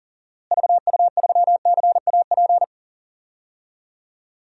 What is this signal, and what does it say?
Morse code, heard as a keyed tone, about 40 wpm, 700 Hz. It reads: VU3CAP
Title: DXCC-100 Call Signs - 6 Characters - Repeated 3x - 40wpm